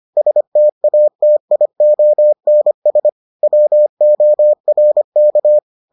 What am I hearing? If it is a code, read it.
Morse code, heard as a keyed tone, about 25 wpm, 600 Hz. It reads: STATIONS WORK